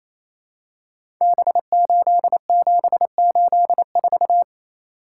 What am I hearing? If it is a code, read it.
Morse code, heard as a keyed tone, about 28 wpm, 700 Hz. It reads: B8784
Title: Code Groups - 5 Letter:Numbers - 28wpm